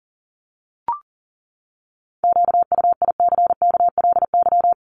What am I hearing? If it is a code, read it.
Morse code, heard as a keyed tone, about 40 wpm, 700 Hz. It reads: QUICKLY
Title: LY Suffix - Top 100 Words with Base Words 40wpm